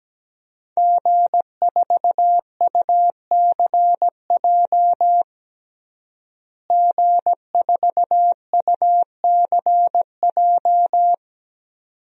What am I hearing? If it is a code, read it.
Morse code, heard as a keyed tone, about 17 wpm, 700 Hz. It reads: G4UCJ G4UCJ
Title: DXCC-100 Call Signs - 5 Characters - Repeated 2x - 17wpm